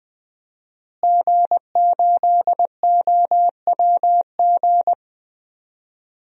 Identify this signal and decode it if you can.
Morse code, heard as a keyed tone, about 20 wpm, 700 Hz. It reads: G8OWG
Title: DXCC-100 Call Signs - 5 Characters - Repeated 3x - 20wpm